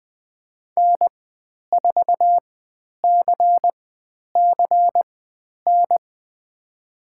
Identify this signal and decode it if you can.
Morse code, heard as a keyed tone, about 20 wpm, spaced as for 10 wpm, 700 Hz. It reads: N4CCN
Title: 102 Lesson 17g - Letter u - Call Signs 20wpm